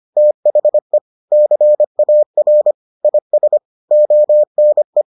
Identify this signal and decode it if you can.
Morse code, heard as a keyed tone, about 25 wpm, 600 Hz. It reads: THE CAR IS ONE